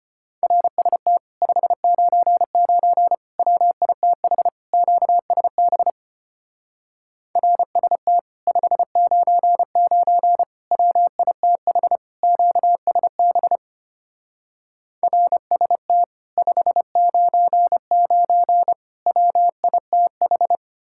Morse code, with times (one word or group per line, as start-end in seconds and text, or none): 0.4-1.2 RST
1.4-3.1 599
3.4-4.5 WITH
4.7-5.9 QSB
7.4-8.2 RST
8.5-10.4 599
10.7-12.0 WITH
12.2-13.6 QSB
15.0-16.0 RST
16.4-18.7 599
19.1-20.6 WITH